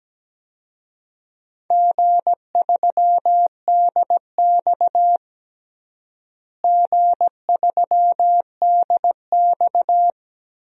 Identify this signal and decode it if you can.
Morse code, heard as a keyed tone, about 17 wpm, 700 Hz. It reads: G3DX G3DX
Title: DXCC-100 Call Signs - 4 Characters - Repeated 2x - 17wpm